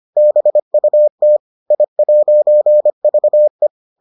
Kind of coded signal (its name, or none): Morse code